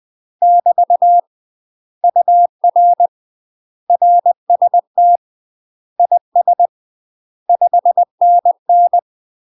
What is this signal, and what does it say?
Morse code, heard as a keyed tone, about 20 wpm, 700 Hz. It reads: <BT> UR RST IS 5NN
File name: UK MASTER QSOs TRAINING / 20WPM / UK MASTER QSOs 0060